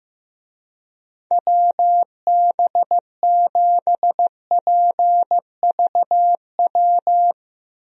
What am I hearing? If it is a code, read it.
Morse code, heard as a keyed tone, about 15 wpm, 700 Hz. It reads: WB7PVW
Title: USA Call Signs - 6 Characters - Repeated 2x - 15wpm